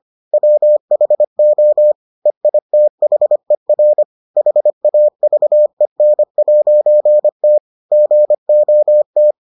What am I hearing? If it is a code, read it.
Morse code, heard as a keyed tone, about 25 wpm, 600 Hz. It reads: WHO EITHER HAVEN'T GOT